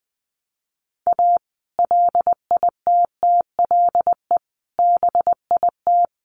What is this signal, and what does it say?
Morse code, heard as a keyed tone, about 20 wpm, 700 Hz. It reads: A LITTLE BIT
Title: Sets of 4 Words 20wpm